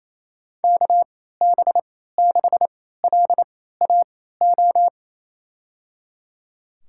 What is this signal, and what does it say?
Morse code, heard as a keyed tone, about 28 wpm, spaced as for 16 wpm, 700 Hz. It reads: KB6LAO
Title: USA Call Signs - 6 Characters - 3x Character Spacing - 28wpm